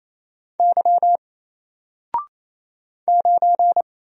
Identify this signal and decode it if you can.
Morse code, heard as a keyed tone, about 27 wpm, 700 Hz. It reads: Y 9
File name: Single-Letter-Number-28wpm